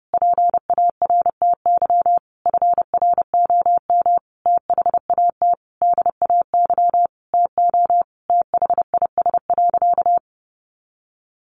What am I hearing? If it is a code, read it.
Morse code, heard as a keyed tone, about 30 wpm, 700 Hz. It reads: PARTY FROM THAT DAY TO THIS.
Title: Sentences from Top 1400 Words 30wpm